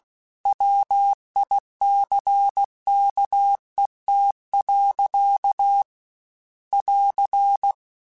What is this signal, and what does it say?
Morse code, heard as a keyed tone, about 16 wpm, 800 Hz. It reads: WICKET. <AR>
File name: Misc / Misc-mp3s / Word Pairs @15